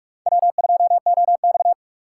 Morse code, heard as a keyed tone, about 44 wpm, 690 Hz. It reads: W2OX
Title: USA Call Signs - 4 Characters - 2x Character Spacing - 45wpm